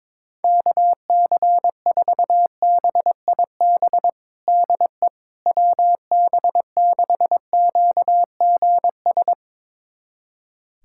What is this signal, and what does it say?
Morse code, heard as a keyed tone, about 22 wpm, 700 Hz. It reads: KC4BIB DE WB6QGS
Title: USA Callsign DE Callsign - 6 Characters - 22wpm